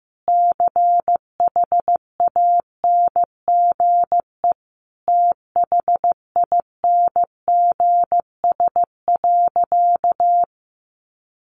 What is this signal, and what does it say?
Morse code, heard as a keyed tone, about 14 wpm, 700 Hz. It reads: CHANGE THINGS.
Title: A Sound Of Thunder 15wpm